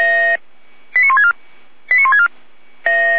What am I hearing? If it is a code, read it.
Five-tone selective call (ZVEI), heard as quick runs of five tones, tones of about 80 ms. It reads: 89164 89164
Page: /455/SDR/ZVEI/